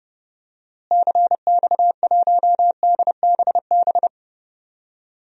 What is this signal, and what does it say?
Morse code, heard as a keyed tone, about 30 wpm, 700 Hz. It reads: CX1DBB
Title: DXCC-100 Call Signs - 6 Characters - Repeated 2x - 30wpm